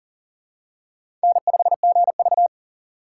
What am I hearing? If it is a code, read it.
Morse code, heard as a keyed tone, about 40 wpm, 700 Hz. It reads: N5GV
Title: USA Call Signs - 4x Character Spacing - 40wpm